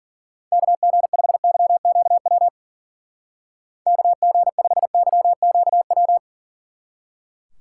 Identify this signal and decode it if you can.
Morse code, heard as a keyed tone, about 49 wpm, 700 Hz. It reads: KG5YQW KG5YQW
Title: Speed Racing - USA 2x3 Call Signs 40wpm